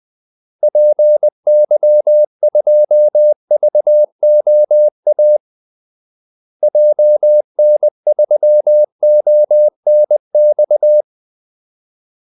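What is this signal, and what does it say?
Morse code, heard as a keyed tone, about 20 wpm, 600 Hz. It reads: PY2VOA JN3ONX
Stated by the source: Morse call-sign practice at 20 wpm.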